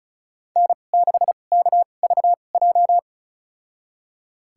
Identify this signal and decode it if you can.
Morse code, heard as a keyed tone, about 34 wpm, spaced as for 25 wpm, 700 Hz. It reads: N6KVJ
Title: USA Call Signs - 2x Character Spacing - 35wpm